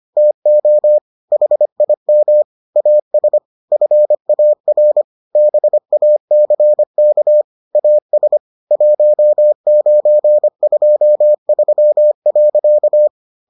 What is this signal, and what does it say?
Morse code, heard as a keyed tone, about 25 wpm, 600 Hz. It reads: TO HIM AS FAR BACK AS 1923.